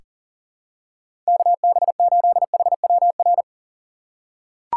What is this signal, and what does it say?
Morse code, heard as a keyed tone, about 40 wpm, 700 Hz. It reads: KB8HWR
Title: USA Call Signs - 6 Characters - 6x Character Spacing - 40wpm